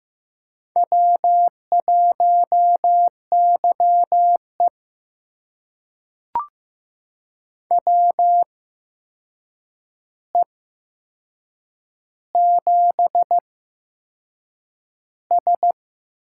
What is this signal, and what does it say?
Morse code, heard as a keyed tone, about 15 wpm, spaced as for 6 wpm, 700 Hz. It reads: W1YE WE7S
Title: USA Call Signs - 4 Characters - 8x Character Spacing - 15wpm